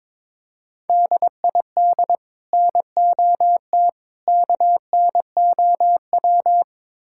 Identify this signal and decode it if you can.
Morse code, heard as a keyed tone, about 22 wpm, 700 Hz. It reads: DID NOT KNOW
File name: practice / 22wpm / Build A Phrase - 5-Word Phrases - 22wpm